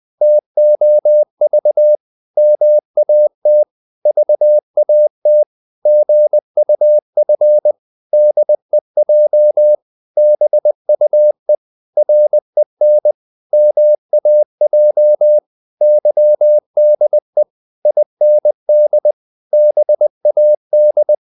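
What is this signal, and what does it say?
Morse code, heard as a keyed tone, about 20 wpm, 600 Hz. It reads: TOV MAT VAT GUF DEJ BUE REN MAJ YDE IND BAD
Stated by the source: Morse code practice of short Danish words at 20 wpm, from CW med Gnister.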